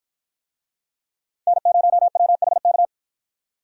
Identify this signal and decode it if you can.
Morse code, heard as a keyed tone, about 50 wpm, 700 Hz. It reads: N0WHK